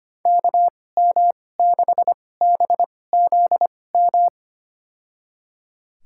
Morse code, heard as a keyed tone, about 25 wpm, spaced as for 18 wpm, 700 Hz. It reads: KM6BZM
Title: USA Call Signs - 6 Characters - 2x Character Spacing - 25wpm